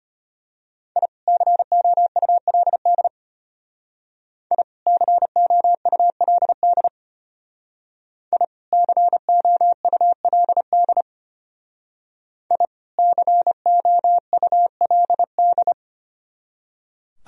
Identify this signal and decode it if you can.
Morse code, heard as a keyed tone, about 33 wpm, 700 Hz. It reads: I COULD I COULD I COULD I COULD